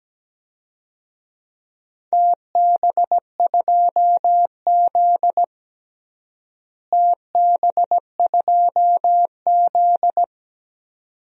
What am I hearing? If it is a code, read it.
Morse code, heard as a keyed tone, about 17 wpm, 700 Hz. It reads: TB2Z TB2Z
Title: DXCC-100 Call Signs - 4 Characters - Repeated 2x - 17wpm